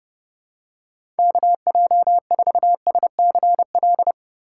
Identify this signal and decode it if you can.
Morse code, heard as a keyed tone, about 32 wpm, 700 Hz. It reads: KJ4SCL
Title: USA Call Signs - 6 Characters - 2x Character Spacing - 30wpm